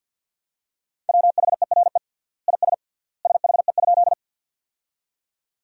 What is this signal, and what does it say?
Morse code, heard as a keyed tone, about 50 wpm, 700 Hz. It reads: WHERE IS SHE?
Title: 3x Word-Spacing - Sentences from Top 200 Words - 50wpm